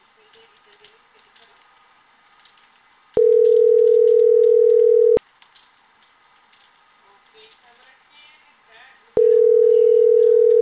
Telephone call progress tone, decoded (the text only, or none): ringback tone